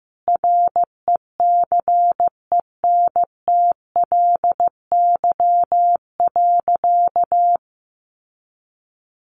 Morse code, heard as a keyed tone, about 15 wpm, 700 Hz. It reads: RECENTLY.
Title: Sentences from Top 1200 Words 15wpm